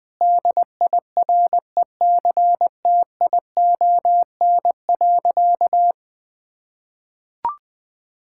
Morse code, heard as a keyed tone, about 20 wpm, 700 Hz. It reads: DIRECTION.
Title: Sentences from Top 1000 Words - Review Part 5 - 20wpm